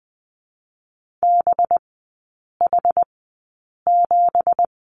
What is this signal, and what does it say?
Morse code, heard as a keyed tone, about 20 wpm, spaced as for 8 wpm, 700 Hz. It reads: BH7
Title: 205 Lesson 33e - Letter j - Three Characters 20wpm